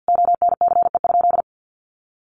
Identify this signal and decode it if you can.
Morse code, heard as a keyed tone, about 50 wpm, 700 Hz. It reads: ONCE?
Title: Sentences from Top 1100 Words 50wpm